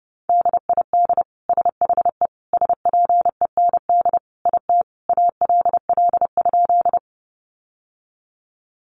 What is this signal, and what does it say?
Morse code, heard as a keyed tone, about 30 wpm, 700 Hz. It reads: DID SHE SPEND IT ALL?
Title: Sentences from Top 800 Words 30wpm